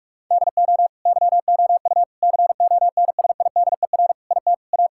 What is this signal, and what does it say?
Morse code, heard as a keyed tone, about 41 wpm, 700 Hz. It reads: DO YOU CONSIDER IT A
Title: Sentences from Top 1000 Words - Review Part 6 - 45wpm